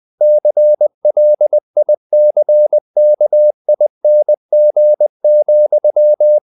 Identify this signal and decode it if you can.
Morse code, heard as a keyed tone, about 20 wpm, 600 Hz. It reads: CLICKING,
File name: the old man and the sea part 2 20wpm